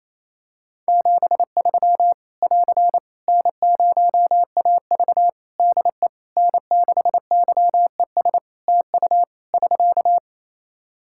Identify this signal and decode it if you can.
Morse code, heard as a keyed tone, about 27 wpm, 700 Hz. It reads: 73 <AR> N0AV DE N6YES TU <SK>
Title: Practice QSO - Exchange 4 - Ending - 28wpm